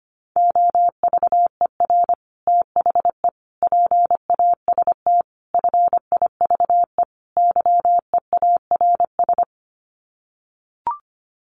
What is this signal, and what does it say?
Morse code, heard as a keyed tone, about 25 wpm, 700 Hz. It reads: OVER THE PAST FIVE YEARS
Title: Sets of 5 Words Encore 25wpm